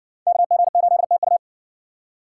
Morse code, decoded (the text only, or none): KG8TU